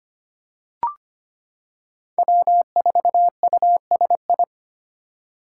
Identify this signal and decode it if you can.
Morse code, heard as a keyed tone, about 25 wpm, 700 Hz. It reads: W4USI
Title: K1USN SST QSO Practice - 25wpm